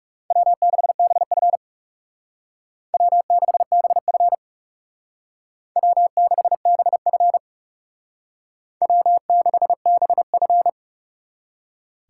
Morse code, heard as a keyed tone, about 37 wpm, 700 Hz. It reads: W6BF W6BF W6BF W6BF
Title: Speed Racing - USA 1x2 Call Signs 30wpm